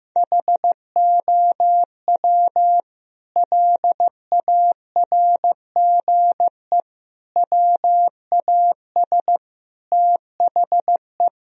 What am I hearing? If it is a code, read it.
Morse code, heard as a keyed tone, about 15 wpm, 700 Hz. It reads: HOW LARGE WAS THE
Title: Sentences from Top 1000 Words - Review Part 4 - 15wpm